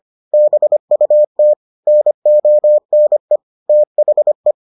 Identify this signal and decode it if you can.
Morse code, heard as a keyed tone, about 25 wpm, 600 Hz. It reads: BUT NONE THE